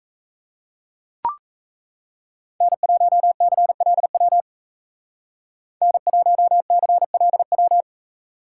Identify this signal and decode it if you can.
Morse code, heard as a keyed tone, about 33 wpm, 700 Hz. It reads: N1CLW N1CLW
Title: Speed Racing - USA Call Signs 28wpm